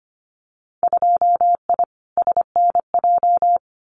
Morse code, heard as a keyed tone, about 25 wpm, 700 Hz. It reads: 2I SNJ
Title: Winter Field Day Exchanges Encore 25wpm